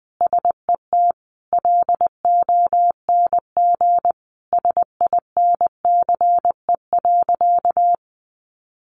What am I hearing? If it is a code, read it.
Morse code, heard as a keyed tone, about 20 wpm, 700 Hz. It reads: SET LONG SINCE.
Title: Sentences from Top 300 Words 20wpm